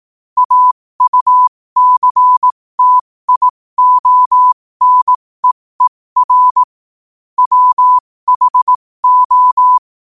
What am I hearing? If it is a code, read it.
Morse code, heard as a keyed tone, about 18 wpm, spaced as for 15 wpm, 1000 Hz. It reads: AUCTIONEER WHO
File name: qotdfiles / 15wpm